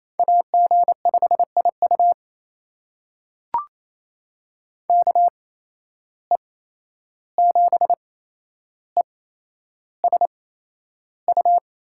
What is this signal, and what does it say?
Morse code, heard as a keyed tone, about 28 wpm, spaced as for 11 wpm, 700 Hz. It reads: AG5IU KE7ESU
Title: USA Call Signs - 8x Character Spacing - 28wpm